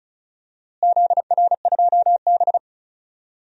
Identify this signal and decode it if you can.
Morse code, heard as a keyed tone, about 34 wpm, 700 Hz. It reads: ZR2B
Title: DXCC-100 Call Signs - 4 Characters - Repeated 3x - 35wpm